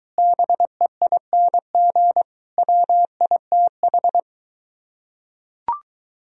Morse code, heard as a keyed tone, about 23 wpm, 700 Hz. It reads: BEING WITH